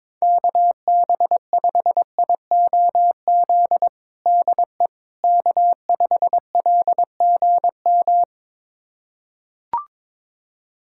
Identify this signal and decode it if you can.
Morse code, heard as a keyed tone, about 22 wpm, 700 Hz. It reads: KB5IOZ DE K5LGM
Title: USA Callsign DE Callsign 22wpm